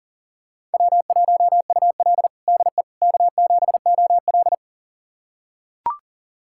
Morse code, heard as a keyed tone, about 40 wpm, 700 Hz. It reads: W1UL DE K7OL
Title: USA Callsign DE Callsign - 4 Characters - 40wpm